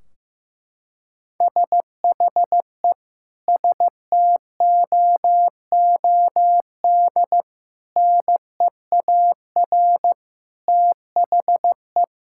Morse code, heard as a keyed tone, about 15 wpm, 700 Hz. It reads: SHE STOOD NEAR THE
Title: Sentences from Top 1600 Words 15wpm